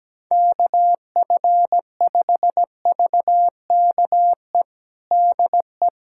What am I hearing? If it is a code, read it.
Morse code, heard as a keyed tone, about 17 wpm, 700 Hz. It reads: KF5VKE DE
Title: USA Callsign DE Callsign - 6 Characters - 17wpm